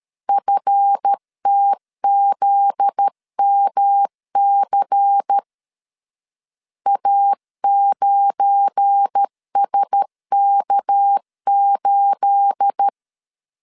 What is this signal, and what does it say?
Morse code, heard as a keyed tone, about 13 wpm, 800 Hz. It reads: FTZMC A9SK8
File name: LEZIONI / 9 28 12wpm 2351